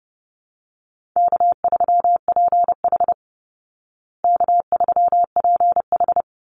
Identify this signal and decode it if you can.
Morse code, heard as a keyed tone, about 30 wpm, 700 Hz. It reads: K3PH K3PH
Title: Easy Call Signs Repeated 3x - 30wpm